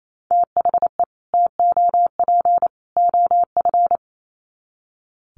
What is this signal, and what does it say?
Morse code, heard as a keyed tone, about 28 wpm, 700 Hz. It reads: THE TOP OF
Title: Sets of 3 Words 28wpm